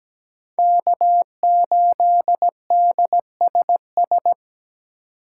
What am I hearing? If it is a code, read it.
Morse code, heard as a keyed tone, about 17 wpm, 700 Hz. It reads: K8DSS
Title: K1USN SST QSO Practice - 17wpm